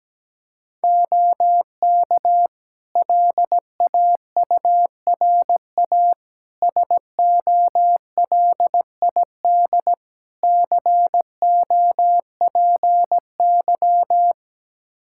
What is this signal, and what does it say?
Morse code, heard as a keyed tone, about 17 wpm, 700 Hz. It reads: OK LAURA SOLID COPY
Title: K1USN SST QSO Practice 17wpm